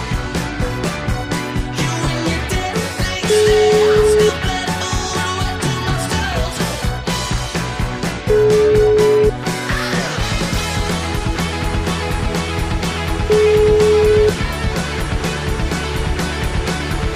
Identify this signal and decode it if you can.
Ringback tone (Europe, 425 Hz), with noise.